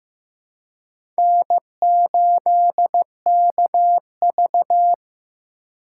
Morse code, heard as a keyed tone, about 15 wpm, 700 Hz. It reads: N8KV
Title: USA Call Signs - 4x Character Spacing - 15wpm